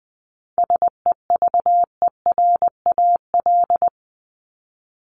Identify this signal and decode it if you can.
Morse code, heard as a keyed tone, about 20 wpm, 700 Hz. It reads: SEVERAL
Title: Top 500 Words 20wpm